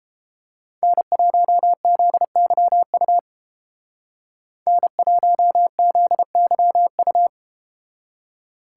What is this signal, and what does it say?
Morse code, heard as a keyed tone, about 31 wpm, 700 Hz. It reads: N1ZYU N1ZYU